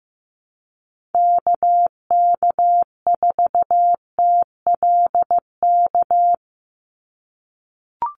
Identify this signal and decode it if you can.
Morse code, heard as a keyed tone, about 15 wpm, 700 Hz. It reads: KK4TLK
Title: USA Call Signs 15wpm